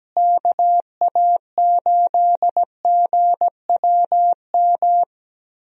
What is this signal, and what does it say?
Morse code, heard as a keyed tone, about 17 wpm, 700 Hz. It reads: KA8GWM
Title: USA Call Signs - 6 Characters - Repeated 2x - 17wpm